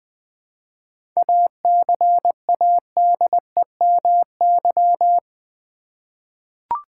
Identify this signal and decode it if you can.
Morse code, heard as a keyed tone, about 20 wpm, 700 Hz. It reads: ACADEMY
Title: 7 Letter Words - First 3 Letters Given - 20wpm